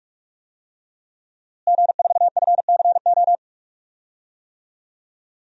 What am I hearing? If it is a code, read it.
Morse code, heard as a keyed tone, about 42 wpm, 700 Hz. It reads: G4FCO
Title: DXCC-100 Call Signs - 5 Characters - Repeated 3x - 45wpm